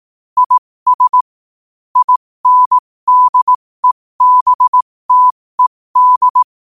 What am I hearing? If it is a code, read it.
Morse code, heard as a keyed tone, about 18 wpm, spaced as for 15 wpm, 1000 Hz. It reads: IS INDEBTED
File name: qotdfiles / 15wpm